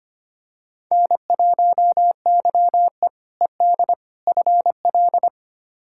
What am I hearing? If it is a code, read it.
Morse code, heard as a keyed tone, about 25 wpm, 700 Hz. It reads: N1YE ED FL
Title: K1USN SST QSO Practice - 25wpm